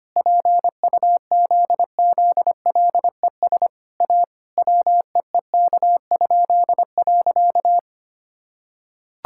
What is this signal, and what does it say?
Morse code, heard as a keyed tone, about 25 wpm, 700 Hz. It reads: PUZZLES A WEEK?.
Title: Pangram Challenge v2 - 25wpm